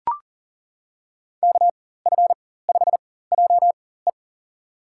Morse code, heard as a keyed tone, about 40 wpm, spaced as for 19 wpm, 700 Hz. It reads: KF5JE
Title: USA Call Signs - 5 Characters - 4x Character Spacing - 40wpm